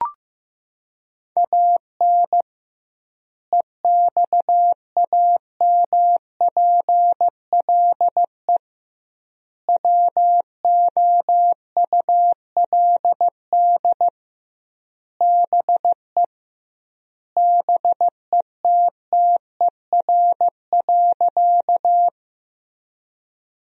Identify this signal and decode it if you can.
Morse code, heard as a keyed tone, about 15 wpm, 700 Hz. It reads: AN EXAMPLE WOULD BE BETTER.
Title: 2x Word-Spacing - Sentences from Top 400 Words - 15wpm